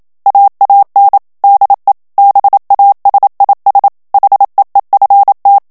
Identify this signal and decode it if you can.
Morse code, heard as a keyed tone, about 27 wpm, 800 Hz. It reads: AAN DE BASIS HEEFT
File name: HOLLANDAIS texte clair 1 vitesse 26 mots minute